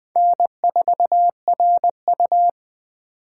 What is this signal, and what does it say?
Morse code, heard as a keyed tone, about 20 wpm, 700 Hz. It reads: N4RU